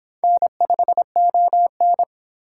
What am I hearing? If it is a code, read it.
Morse code, heard as a keyed tone, about 26 wpm, 700 Hz. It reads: N5ON